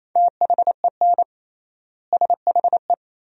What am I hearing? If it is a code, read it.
Morse code, heard as a keyed tone, about 28 wpm, 700 Hz. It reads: THEN SHE
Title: 3x Word-Spacing - Sentences from Top 400 Words - 28wpm